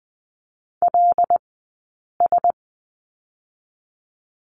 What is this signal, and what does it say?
Morse code, heard as a keyed tone, about 20 wpm, 700 Hz. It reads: L S
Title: 073 Lesson 13d - Letter L - Two Characters 20wpm